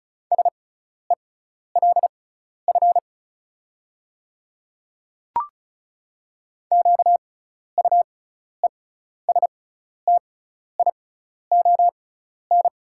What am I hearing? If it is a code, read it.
Morse code, heard as a keyed tone, about 34 wpm, spaced as for 12 wpm, 700 Hz. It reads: SELF QUESTION
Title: 6x Character Spacing - Top 500 Words - 35wpm